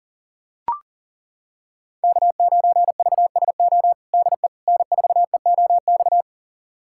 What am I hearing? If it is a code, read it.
Morse code, heard as a keyed tone, about 40 wpm, 700 Hz. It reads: K9VSO DE N4EOX
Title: USA Callsign DE Callsign - 5 Characters - 40wpm